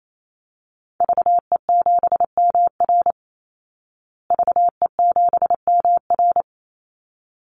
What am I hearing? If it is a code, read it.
Morse code, heard as a keyed tone, about 28 wpm, 700 Hz. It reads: VE7MR VE7MR
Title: Call Signs Repeated 3x - 28wpm